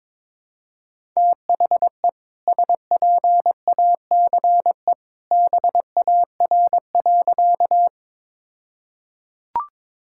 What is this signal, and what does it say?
Morse code, heard as a keyed tone, about 22 wpm, 700 Hz. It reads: THE SPACE BAR.
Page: morse-code-ninja-practice-22wpm